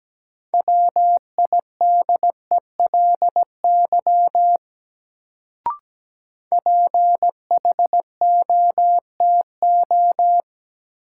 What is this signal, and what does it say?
Morse code, heard as a keyed tone, about 17 wpm, 700 Hz. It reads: WIDELY PHOTO
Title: Sentences from Top 1200 Words - New Word Drill - 17wpm